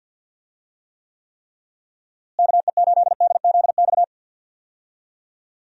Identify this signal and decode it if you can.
Morse code, heard as a keyed tone, about 47 wpm, 700 Hz. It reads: KE8DZX
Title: USA Call Signs - 6 Characters - Repeated 3x - 50wpm